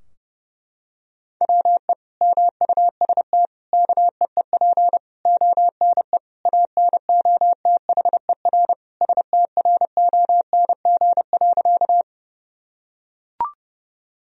Morse code, heard as a keyed tone, about 30 wpm, 700 Hz. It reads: WE MUST KEEP ONE ANOTHER STRONG.
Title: Sentences from Top 1000 Words - Review Part 3 - 30wpm